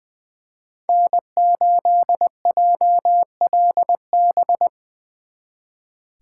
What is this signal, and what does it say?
Morse code, heard as a keyed tone, about 23 wpm, 700 Hz. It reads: N8JLB
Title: Speed Racing - USA 1x3 Call Signs 20wpm